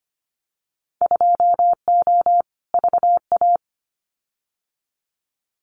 Morse code, heard as a keyed tone, about 25 wpm, 690 Hz. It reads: 2O VA
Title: Winter Field Day Exchanges 25wpm